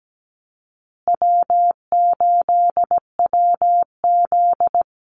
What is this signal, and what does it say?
Morse code, heard as a keyed tone, about 17 wpm, 700 Hz. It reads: W8WZ
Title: Easy Call Signs Repeated 3x - 17wpm